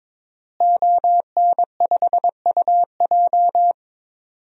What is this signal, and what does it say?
Morse code, heard as a keyed tone, about 22 wpm, 700 Hz. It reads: ON5UJ
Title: DXCC-100 Call Signs - 5 Characters - Repeated 2x - 22wpm